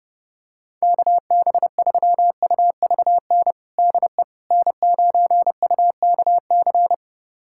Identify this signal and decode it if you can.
Morse code, heard as a keyed tone, about 30 wpm, 700 Hz. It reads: KB3UVN DE N9UKC
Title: USA Callsign DE Callsign 30wpm